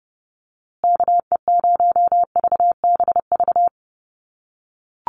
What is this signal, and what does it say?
Morse code, heard as a keyed tone, about 30 wpm, 700 Hz. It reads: KE0VBV
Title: USA Call Signs Repeated 3x 30wpm